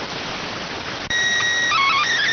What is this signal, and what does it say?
SSTV (slow-scan television), PD 290.